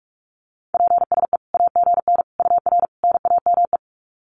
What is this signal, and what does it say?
Morse code, heard as a keyed tone, about 44 wpm, 700 Hz. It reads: PSE AGN UR NAME
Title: Common QSO Phrases 45wpm